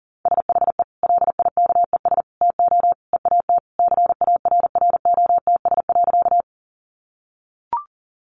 Morse code, heard as a keyed tone, about 40 wpm, 700 Hz. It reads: SHE LIKES TO EAT CARROTS.
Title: Easy Phrases 40wpm